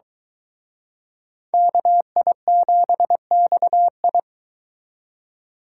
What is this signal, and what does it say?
Morse code, heard as a keyed tone, about 22 wpm, 700 Hz. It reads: KI7XI